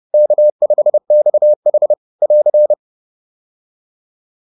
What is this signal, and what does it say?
Morse code, heard as a keyed tone, about 30 wpm, 600 Hz. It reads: K5XH <AR>